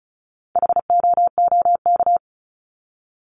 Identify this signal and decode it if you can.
Morse code, heard as a keyed tone, about 34 wpm, 700 Hz. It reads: HOOK
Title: Top 1501-1750 Words 35wpm